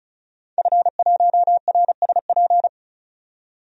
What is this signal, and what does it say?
Morse code, heard as a keyed tone, about 34 wpm, 700 Hz. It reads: F1RSP
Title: DXCC-100 Call Signs - 5 Characters - 35wpm